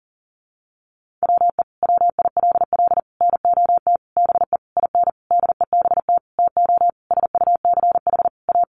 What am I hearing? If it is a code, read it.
Morse code, heard as a keyed tone, about 37 wpm, 700 Hz. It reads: WE WILL NOT BE IN DEBT TO SUCH A
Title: Sentences from Top 1300 Words 40wpm